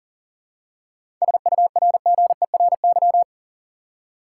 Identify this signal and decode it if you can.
Morse code, heard as a keyed tone, about 40 wpm, 700 Hz. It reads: SURGERY
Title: Seven Letter Words - 40wpm